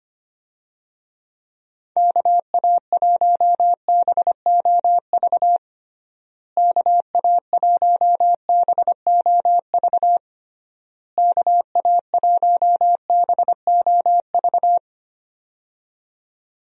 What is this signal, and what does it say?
Morse code, heard as a keyed tone, about 25 wpm, 700 Hz. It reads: KA1BOV KA1BOV KA1BOV
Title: USA Call Signs - 6 Characters - Repeated 3x - 25wpm